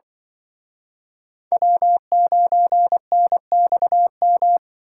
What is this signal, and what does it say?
Morse code, heard as a keyed tone, about 24 wpm, 700 Hz. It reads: W9NXM
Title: Speed Racing - USA Call Signs 20wpm